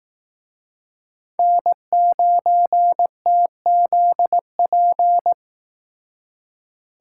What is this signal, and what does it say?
Morse code, heard as a keyed tone, about 18 wpm, 700 Hz. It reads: N9TZP
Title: Speed Racing - USA 1x3 Call Signs 15wpm